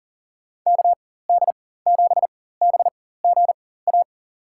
Morse code, heard as a keyed tone, about 40 wpm, spaced as for 19 wpm, 700 Hz. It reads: KD7BGA
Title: USA Call Signs - 6 Characters - 4x Character Spacing - 40wpm